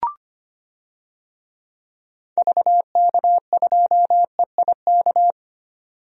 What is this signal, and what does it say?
Morse code, heard as a keyed tone, about 25 wpm, 700 Hz. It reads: VK2EIK
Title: DXCC-100 Call Signs - 6 Characters - 25wpm